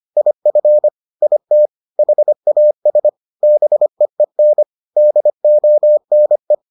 Morse code, heard as a keyed tone, about 25 wpm, 600 Hz. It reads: IF IT HAS BEEN DONE